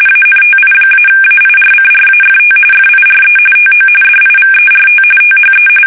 RTTY, about 45.45 baud. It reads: ANTI-AIRCRAFT FIRE TO DIVERT THE
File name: fsk850 / battleshipnj